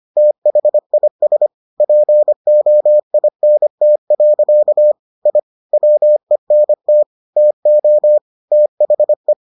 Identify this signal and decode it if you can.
Morse code, heard as a keyed tone, about 25 wpm, 600 Hz. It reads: THIS POINT. I WENT TO THE